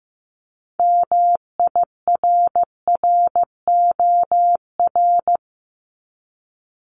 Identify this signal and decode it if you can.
Morse code, heard as a keyed tone, about 15 wpm, 700 Hz. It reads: MIRROR
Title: Multi-letter Phonemes 15wpm